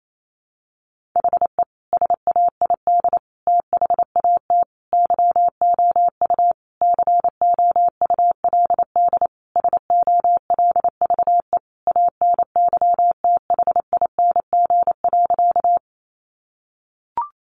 Morse code, heard as a keyed tone, about 28 wpm, 700 Hz. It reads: HE SAID THAT YOU COULD SOLVE ANYTHING.
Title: Sentences from Top 1000 Words 28wpm